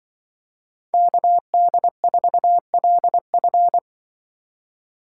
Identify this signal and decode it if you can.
Morse code, heard as a keyed tone, about 23 wpm, 700 Hz. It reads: KD4LF
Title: Speed Racing - USA 2x2 Call Signs 20wpm